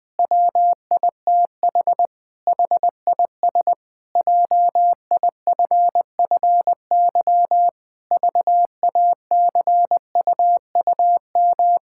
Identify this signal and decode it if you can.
Morse code, heard as a keyed tone, about 21 wpm, 700 Hz. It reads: WITH HIS JIFFY VACUUM
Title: Pangram Challenge v2 - 20wpm